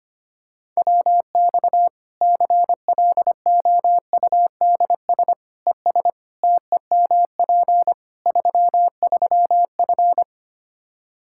Morse code, heard as a keyed tone, about 25 wpm, 700 Hz. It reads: WX CLOUDS ES TEMP 33F
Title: K1USN SST QSO Practice 25wpm